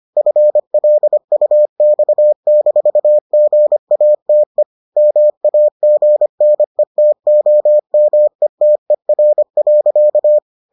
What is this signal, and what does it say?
Morse code, heard as a keyed tone, about 25 wpm, 600 Hz. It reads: FLUX-GATE MAGNETOMETER.